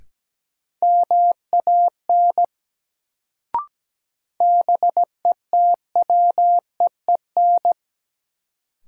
Morse code, heard as a keyed tone, about 17 wpm, 700 Hz. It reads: MAN BETWEEN
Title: Sets of 2 Words - New Word Drill - 17wpm